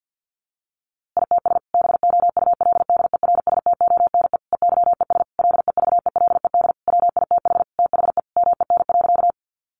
Morse code, heard as a keyed tone, about 50 wpm, 700 Hz. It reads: ITS BOULDERSTONE EYES LEVELED WITH THE MEN.
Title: A Sound Of Thunder 50wpm